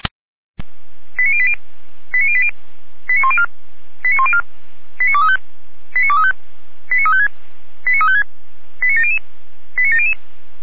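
Five-tone selective call (ZVEI), heard as quick runs of five tones, tones of about 70 ms. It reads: 89080 89080 89104 89104 89235 89235 89356 89356 89700 89700
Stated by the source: ZVEI five-tone call received by SDR.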